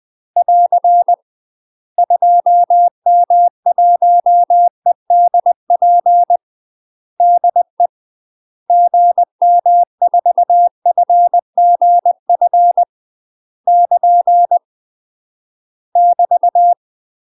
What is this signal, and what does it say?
Morse code, heard as a keyed tone, about 20 wpm, 700 Hz. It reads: <AR> 2M1EDP DE GM4FGF <KN> <BT>
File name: UK MASTER QSOs TRAINING / 20WPM / UK MASTER QSOs 0047